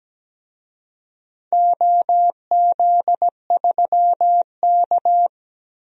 Morse code, heard as a keyed tone, about 17 wpm, 700 Hz. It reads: OZ3K
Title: DXCC-100 Call Signs - 4 Characters - Repeated 3x - 17wpm